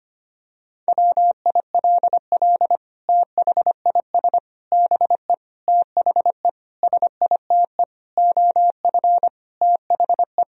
Morse code, heard as a keyed tone, about 25 wpm, 700 Hz. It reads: WILL THIS BE THE SITE OF THE
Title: Sentences from Top 1000 Words - Review Part 1 - 25wpm